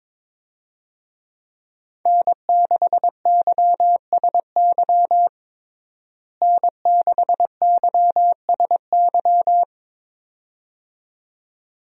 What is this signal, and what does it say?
Morse code, heard as a keyed tone, about 22 wpm, 700 Hz. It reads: N6YSY N6YSY
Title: USA Call Signs - 5 Characters - Repeated 2x - 22wpm